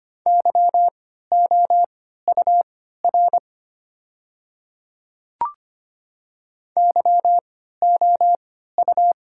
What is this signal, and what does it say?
Morse code, heard as a keyed tone, about 25 wpm, spaced as for 14 wpm, 700 Hz. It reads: YOUR YOU
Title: 3x Character Spacing - Top 100 Words - 25wpm